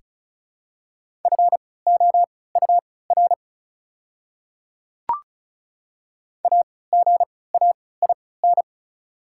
Morse code, heard as a keyed tone, about 34 wpm, spaced as for 20 wpm, 700 Hz. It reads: FOUR AGAIN
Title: 3x Character Spacing - Top 500 Words - 35wpm